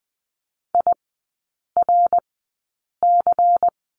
Morse code, heard as a keyed tone, about 20 wpm, spaced as for 8 wpm, 700 Hz. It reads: IRC
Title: 093 Lesson 16e - Letter c - Three Characters 20wpm